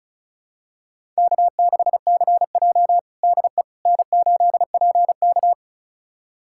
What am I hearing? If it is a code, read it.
Morse code, heard as a keyed tone, about 36 wpm, 690 Hz. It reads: K6CJ DE N8PK
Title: USA Callsign DE Callsign - 4 Characters - 35wpm